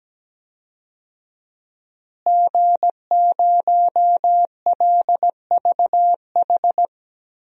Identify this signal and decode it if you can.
Morse code, heard as a keyed tone, about 17 wpm, 700 Hz. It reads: G0LVH
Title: DXCC-100 Call Signs - 5 Characters - Repeated 2x - 17wpm